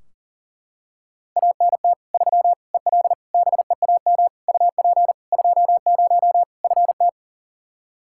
Morse code, heard as a keyed tone, about 37 wpm, 700 Hz. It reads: ANT 3 EL BEAM UP 20 FT
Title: Practice QSO - Exchange 2 - Info - 40wpm